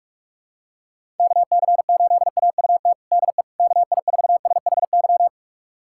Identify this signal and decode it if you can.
Morse code, heard as a keyed tone, about 41 wpm, 700 Hz. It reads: KC8AUT DE KI4SHY
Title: USA Callsign DE Callsign - 6 Characters - 45wpm